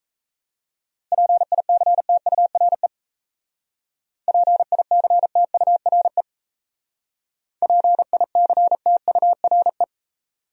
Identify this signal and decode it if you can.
Morse code, heard as a keyed tone, about 33 wpm, 700 Hz. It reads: PICTURE PICTURE PICTURE
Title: Speed-Racing Top 300 Words 28wpm